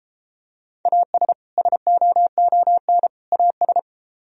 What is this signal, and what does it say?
Morse code, heard as a keyed tone, about 30 wpm, 700 Hz. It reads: AS SOON AS